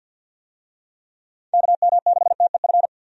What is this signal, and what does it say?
Morse code, heard as a keyed tone, about 51 wpm, 700 Hz. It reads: KM6TEF
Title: USA Call Signs - 6 Characters - 50wpm